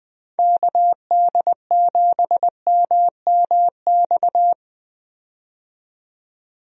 Morse code, heard as a keyed tone, about 20 wpm, 700 Hz. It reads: KD7MMX
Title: USA Call Signs - 6 Characters - 20wpm